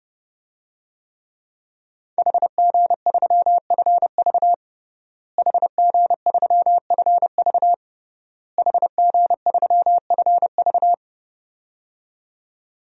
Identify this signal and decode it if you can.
Morse code, heard as a keyed tone, about 30 wpm, 700 Hz. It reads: HG3FV HG3FV HG3FV
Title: DXCC-100 Call Signs - 5 Characters - Repeated 3x - 30wpm